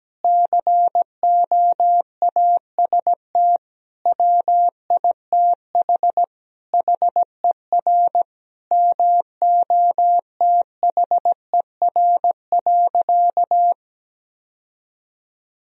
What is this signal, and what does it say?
Morse code, heard as a keyed tone, about 17 wpm, 700 Hz. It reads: COAST WITH HER MOTHER.
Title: Sentences from Top 1000 Words - Review Part 4 - 17wpm